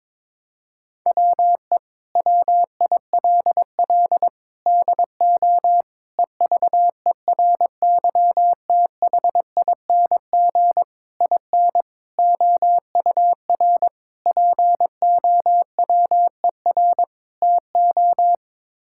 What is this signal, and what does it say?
Morse code, heard as a keyed tone, about 22 wpm, 700 Hz. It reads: WE WILL DO EVERYTHING IN OUR POWER TO
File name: practice / 22wpm / Sets of 8 Words - 22wpm